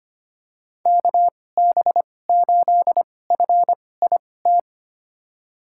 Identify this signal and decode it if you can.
Morse code, heard as a keyed tone, about 25 wpm, spaced as for 18 wpm, 700 Hz. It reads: KB8FIT
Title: USA Call Signs - 6 Characters - 2x Character Spacing - 25wpm